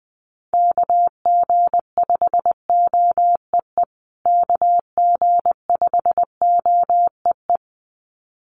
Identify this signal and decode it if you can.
Morse code, heard as a keyed tone, about 20 wpm, 700 Hz. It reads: KG5OEE KG5OEE
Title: USA Call Signs Repeated 2x 20wpm